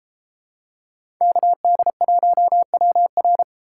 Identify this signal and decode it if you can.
Morse code, heard as a keyed tone, about 30 wpm, 700 Hz. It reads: KD1WR